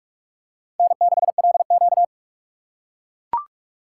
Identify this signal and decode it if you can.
Morse code, heard as a keyed tone, about 42 wpm, 700 Hz. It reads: N6LQ
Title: USA Call Signs - 4 Characters - 3x Character Spacing - 45wpm